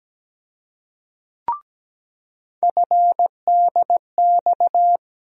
Morse code, heard as a keyed tone, about 17 wpm, 700 Hz. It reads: FDX
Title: USA Call Sign Postfixes - 3 Letters 17wpm